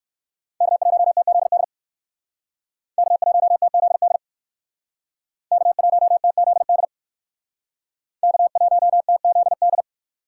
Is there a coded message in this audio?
Yes: Morse code from 0.6 to 9.8 s, heard as a keyed tone, about 57 wpm, 700 Hz. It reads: K1TZD K1TZD K1TZD K1TZD